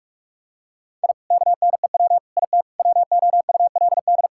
Morse code, heard as a keyed tone, about 46 wpm, 700 Hz. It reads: I KNEW IT WOULD
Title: Sentences from Top 1000 Words - Review Part 5 - 45wpm